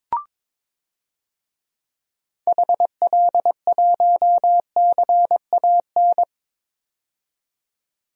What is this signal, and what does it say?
Morse code, heard as a keyed tone, about 22 wpm, 700 Hz. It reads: HL1CAN
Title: DXCC-100 Call Signs - 6 Characters - 22wpm